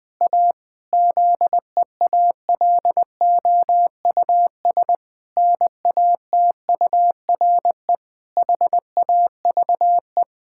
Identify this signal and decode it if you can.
Morse code, heard as a keyed tone, about 20 wpm, 700 Hz. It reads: A ZEALOUS NATURE HAVE
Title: Pangram Challenge v2 - 20wpm